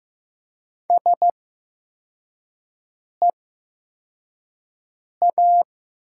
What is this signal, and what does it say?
Morse code, heard as a keyed tone, about 15 wpm, spaced as for 4 wpm, 700 Hz. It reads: SEA
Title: 8x Character Spacing - Top 300 Words - 15wpm